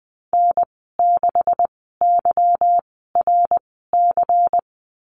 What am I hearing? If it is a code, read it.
Morse code, heard as a keyed tone, about 19 wpm, spaced as for 14 wpm, 700 Hz. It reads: N6YRC
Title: 147 Lesson 24g - Letter y - Call Signs 20wpm